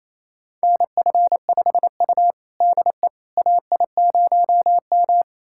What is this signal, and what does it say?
Morse code, heard as a keyed tone, about 27 wpm, 700 Hz. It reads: NF5U DE AI0M
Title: USA Callsign DE Callsign - 4 Characters - 28wpm